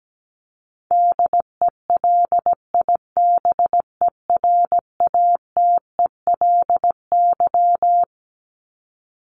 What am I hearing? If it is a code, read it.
Morse code, heard as a keyed tone, about 17 wpm, 700 Hz. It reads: DELIBERATELY
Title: Top 1751-2000 Words 17wpm